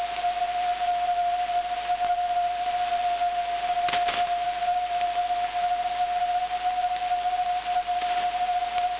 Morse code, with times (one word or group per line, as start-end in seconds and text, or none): none